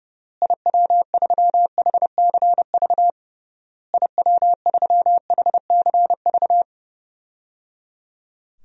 Morse code, heard as a keyed tone, about 30 wpm, 700 Hz. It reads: IW3HCV IW3HCV
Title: DXCC-100 Call Signs - 6 Characters - Repeated 3x - 30wpm